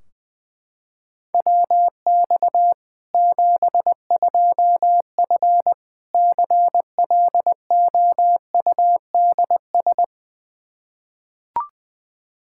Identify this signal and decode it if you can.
Morse code, heard as a keyed tone, about 21 wpm, 700 Hz. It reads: WX 72F CLOUDS
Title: K1USN SST QSO Practice 20wpm